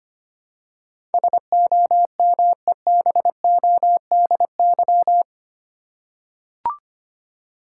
Morse code, heard as a keyed tone, about 25 wpm, 700 Hz. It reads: SOMEBODY
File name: 8 Letter Words - First 3 Letters Given - 25wpm